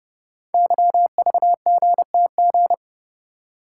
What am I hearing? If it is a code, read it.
Morse code, heard as a keyed tone, about 30 wpm, 700 Hz. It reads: YVGTG
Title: Code Groups - 5 Letters - 30wpm